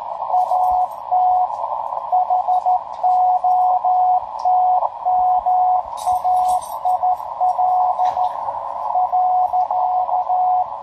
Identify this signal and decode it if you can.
Morse code, heard as a keyed tone, about 13 wpm, 730 Hz, with radio noise. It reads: AT HOTMAIL .
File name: tracks / hp1rcp